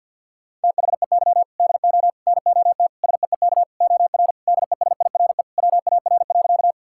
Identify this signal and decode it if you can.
Morse code, heard as a keyed tone, about 50 wpm, 700 Hz. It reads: THEY DO NOT SEEK OR DESIRE WAR.
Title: Sentences from Top 1600 Words 50wpm